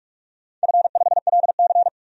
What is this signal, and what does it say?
Morse code, heard as a keyed tone, about 45 wpm, 700 Hz. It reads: F5LC
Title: DXCC-100 Call Signs - 4 Characters - 45wpm